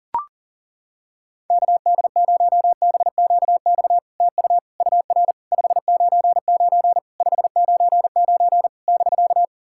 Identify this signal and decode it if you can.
Morse code, heard as a keyed tone, about 40 wpm, 700 Hz. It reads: KD0BQX TU UR 599 599 <BK>
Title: SOTA QSO Practice 40wpm